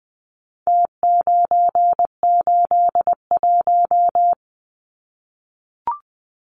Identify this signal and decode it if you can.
Morse code, heard as a keyed tone, about 20 wpm, 700 Hz. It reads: T981
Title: Numbers - Contest Style - 4-Digit 20wpm